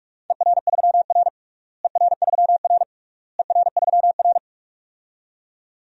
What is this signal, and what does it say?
Morse code, heard as a keyed tone, about 46 wpm, 700 Hz. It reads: ER3R ER3R ER3R
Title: DXCC-100 Call Signs - 4 Characters - Repeated 3x - 45wpm